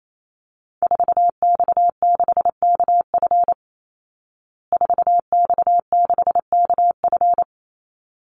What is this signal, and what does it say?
Morse code, heard as a keyed tone, about 28 wpm, 700 Hz. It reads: 4X6KF 4X6KF
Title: Call Signs Repeated 3x - 28wpm